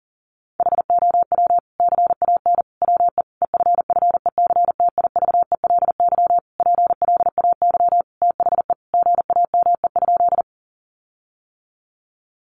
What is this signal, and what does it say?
Morse code, heard as a keyed tone, about 40 wpm, 700 Hz. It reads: HOW CAN WE EFFECTIVELY PLAY THE GAME?
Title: Sentences from Top 1400 Words 40wpm